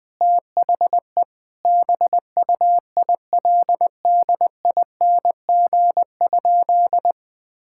Morse code, heard as a keyed tone, about 21 wpm, 700 Hz. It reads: THE BUILDING?
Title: Sentences from Top 1000 Words - Review Part 6 - 20wpm